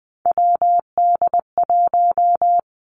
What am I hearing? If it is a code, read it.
Morse code, heard as a keyed tone, about 20 wpm, 700 Hz. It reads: WD1
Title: Three Challenging Letter-Number Combinations 20wpm